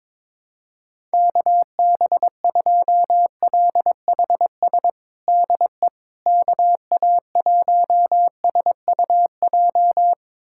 Morse code, heard as a keyed tone, about 22 wpm, 700 Hz. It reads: KB2LHS DE KA1SUJ
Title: USA Callsign DE Callsign 22wpm